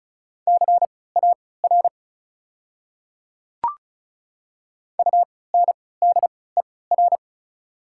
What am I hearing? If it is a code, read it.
Morse code, heard as a keyed tone, about 36 wpm, spaced as for 20 wpm, 690 Hz. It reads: CAR UNDER
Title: 3x Character Spacing - Top 300 Words - 35wpm